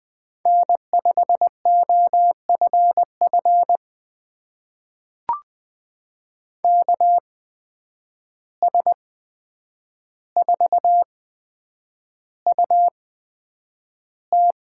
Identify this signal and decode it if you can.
Morse code, heard as a keyed tone, about 20 wpm, spaced as for 9 wpm, 700 Hz. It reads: N5OFF KS4UT
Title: USA Call Signs - 5 Characters - 8x Character Spacing - 20wpm